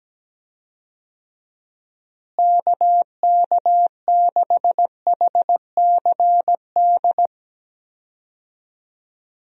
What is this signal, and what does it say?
Morse code, heard as a keyed tone, about 17 wpm, 700 Hz. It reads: KK6HCD
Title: USA Call Signs - 6 Characters - Repeated 3x - 17wpm